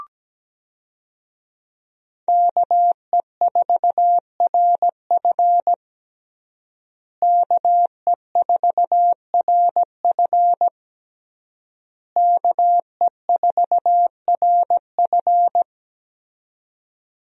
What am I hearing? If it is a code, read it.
Morse code, heard as a keyed tone, about 17 wpm, 700 Hz. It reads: KE4RF KE4RF KE4RF
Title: USA Call Signs - 5 Characters - Repeated 3x - 17wpm